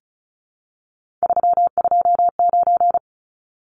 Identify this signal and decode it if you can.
Morse code, heard as a keyed tone, about 34 wpm, 700 Hz. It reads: 329
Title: Numbers - 3-Digit 35wpm